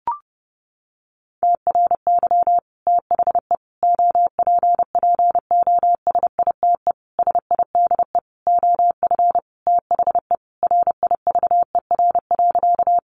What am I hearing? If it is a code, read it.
Morse code, heard as a keyed tone, about 30 wpm, 700 Hz. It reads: TRY THE OPPOSITE SIDE OF THE RIVER.
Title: Sentences from Top 1300 Words 30wpm